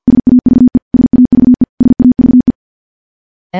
Morse code, heard as a keyed tone, about 50 wpm, 250 Hz. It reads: SAVE SAVE SAVE
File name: morse / morse-training-50WPM